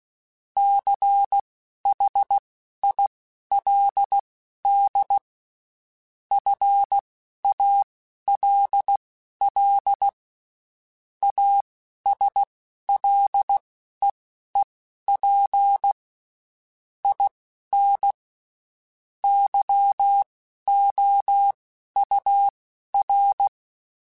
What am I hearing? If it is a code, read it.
Morse code, heard as a keyed tone, about 16 wpm, spaced as for 11 wpm, 800 Hz. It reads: CHILD FALL ASLEEP IN YOUR
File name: Quotes Spaced / Quotes Spaced-mp3s / ChildAsleep sp @15